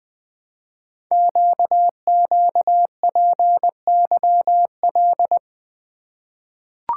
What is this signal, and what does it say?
Morse code, heard as a keyed tone, about 20 wpm, 700 Hz. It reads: QQPYL
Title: Code Groups - 5 Letters - 20wpm